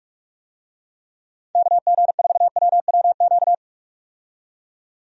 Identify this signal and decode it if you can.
Morse code, heard as a keyed tone, about 42 wpm, 700 Hz. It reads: KG4WWQ
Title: USA Call Signs - 6 Characters - 45wpm